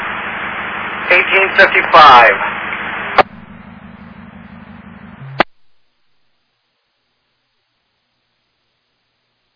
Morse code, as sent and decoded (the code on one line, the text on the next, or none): none